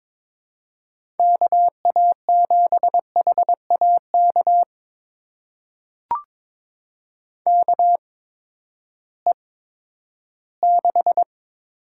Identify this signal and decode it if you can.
Morse code, heard as a keyed tone, about 21 wpm, spaced as for 12 wpm, 700 Hz. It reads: KA7HAK KE6
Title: USA Call Signs - 6 Characters - 8x Character Spacing - 22wpm